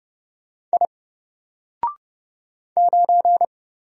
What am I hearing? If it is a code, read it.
Morse code, heard as a keyed tone, about 30 wpm, 700 Hz. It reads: I 9
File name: Single-Letter-Number-30wpm